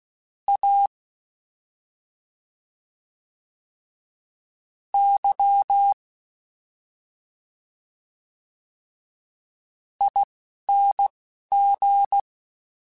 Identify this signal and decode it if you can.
Morse code, heard as a keyed tone, about 16 wpm, spaced as for 4 wpm, 800 Hz. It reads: AYING
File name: Special INGs-mp3s / Special INGs-mp3s / Slow ing @10